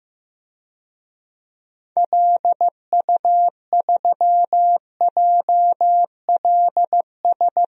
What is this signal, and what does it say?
Morse code, heard as a keyed tone, about 15 wpm, 700 Hz. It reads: LU3JLS
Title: DXCC-100 Call Signs - 6 Characters - Repeated 3x - 15wpm